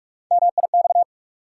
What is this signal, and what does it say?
Morse code, heard as a keyed tone, about 46 wpm, 700 Hz. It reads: MIX
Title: Sentences from Top 1000 Words - Review Part 4 - New Word Drill - 45wpm